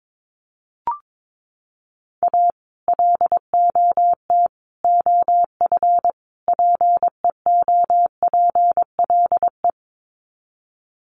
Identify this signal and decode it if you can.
Morse code, heard as a keyed tone, about 22 wpm, 700 Hz. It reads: A LOT OF PEOPLE
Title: Sets of 4 Words 22wpm